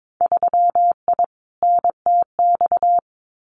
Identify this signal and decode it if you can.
Morse code, heard as a keyed tone, about 22 wpm, 700 Hz. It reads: 3I NTX
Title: Winter Field Day Exchanges 22wpm